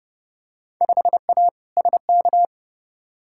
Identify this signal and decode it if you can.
Morse code, heard as a keyed tone, about 29 wpm, 700 Hz. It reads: 5A SK
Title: Field Day Exchanges Encore - 30wpm